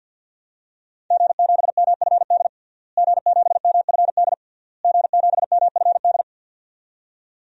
Morse code, heard as a keyed tone, about 52 wpm, 700 Hz. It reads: G7MFD G7MFD G7MFD
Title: DXCC-100 Call Signs - 5 Characters - Repeated 3x - 50wpm